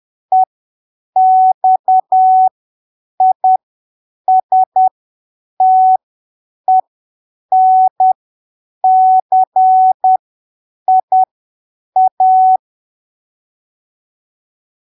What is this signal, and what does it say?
Morse code, heard as a keyed tone, about 10 wpm, spaced as for 7 wpm, 750 Hz. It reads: EXISTENCIA
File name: arquivos / texto / texto 27 10ppm